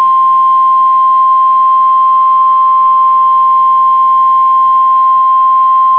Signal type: Emergency Alert System (SAME) tones